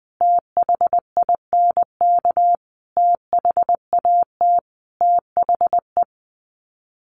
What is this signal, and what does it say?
Morse code, heard as a keyed tone, about 19 wpm, 700 Hz. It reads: THINK THAT THE
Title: Sets of 5 Words Encore 20wpm